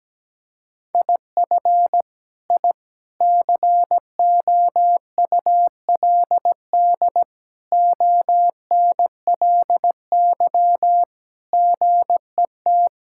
Morse code, heard as a keyed tone, about 17 wpm, 700 Hz. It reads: IF I COULD ONLY GET
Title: Sentences from Top 1000 Words - Review Part 6 - 17wpm